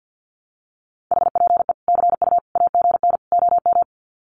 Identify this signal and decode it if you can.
Morse code, heard as a keyed tone, about 50 wpm, 700 Hz. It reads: HPE CU AGN OM
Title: Common QSO Phrases 50wpm